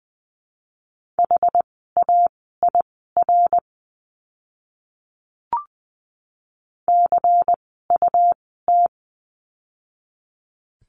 Morse code, heard as a keyed tone, about 20 wpm, spaced as for 14 wpm, 700 Hz. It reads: HAIR CUT
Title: 101 Lesson 17f - Letter u - Words 20wpm